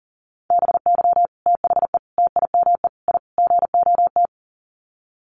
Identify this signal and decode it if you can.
Morse code, heard as a keyed tone, about 40 wpm, 700 Hz. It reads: BY THE TIME I GOT
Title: Sets of 5 Words Encore2 40wpm